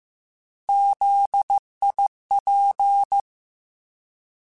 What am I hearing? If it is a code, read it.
Morse code, heard as a keyed tone, about 15 wpm, 800 Hz. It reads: ZIP
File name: Alphabetical / Alphabetical mp3s / Zs @13